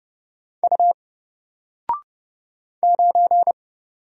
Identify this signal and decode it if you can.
Morse code, heard as a keyed tone, about 30 wpm, 700 Hz. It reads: U 9
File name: Single-Letter-Number-30wpm